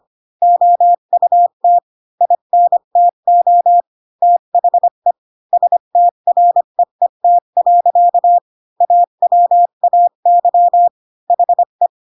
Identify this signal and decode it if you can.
Morse code, heard as a keyed tone, about 25 wpm, 700 Hz. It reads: OUT INTO THE STREET. AWAY HE
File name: Pinocchio / Pinocchio ch3 20 25 ik1hge